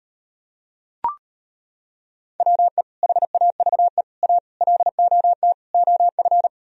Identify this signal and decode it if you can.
Morse code, heard as a keyed tone, about 38 wpm, 700 Hz. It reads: WE HAVE A LOT OF